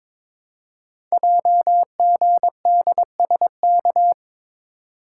Morse code, heard as a keyed tone, about 23 wpm, 700 Hz. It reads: JGDSK
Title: Code Groups - 5 Letter:Numbers - 22wpm